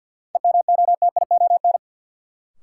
Morse code, heard as a keyed tone, about 52 wpm, 700 Hz. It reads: EMOTION
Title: Sentences from Top 1600 Words - New Word Drill Alt - 50wpm